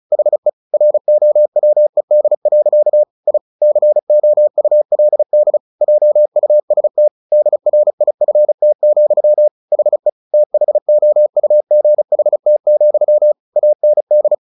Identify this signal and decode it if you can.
Morse code, heard as a keyed tone, about 35 wpm, 600 Hz. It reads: HE ROWED. I COULD JUST DRIFT, HE THOUGHT, AND
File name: the old man and the sea / the old man and the sea 1 35wpm